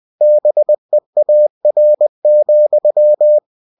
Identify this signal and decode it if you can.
Morse code, heard as a keyed tone, about 20 wpm, 600 Hz. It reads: BEAR,
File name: cw-books / winnie-the-pooh / winnie-20wpm-page-0014